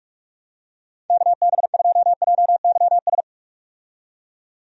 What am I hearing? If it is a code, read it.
Morse code, heard as a keyed tone, about 46 wpm, 700 Hz. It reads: KB2JYS
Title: USA Call Signs - 6 Characters - 45wpm